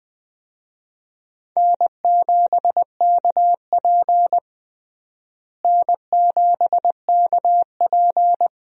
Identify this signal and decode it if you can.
Morse code, heard as a keyed tone, about 20 wpm, 700 Hz. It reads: N7KP N7KP
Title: USA Call Signs - 4 Characters - Repeated 2x - 20wpm